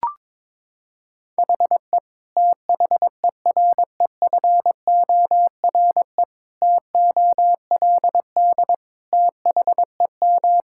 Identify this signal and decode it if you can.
Morse code, heard as a keyed tone, about 22 wpm, 700 Hz. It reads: HE THEREFORE TOLD THEM
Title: Sentences from Top 1000 Words - Review Part 5 - 22wpm